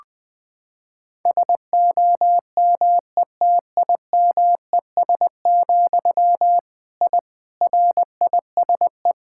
Morse code, heard as a keyed tone, about 21 wpm, 700 Hz. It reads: SOMETIMES, I RISE
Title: Sentences from Top 1000 Words - Review Part 6 - 20wpm